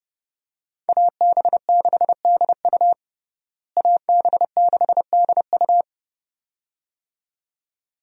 Morse code, heard as a keyed tone, about 30 wpm, 700 Hz. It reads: AB6DU AB6DU
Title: USA Call Signs - 5 Characters - Repeated 2x - 30wpm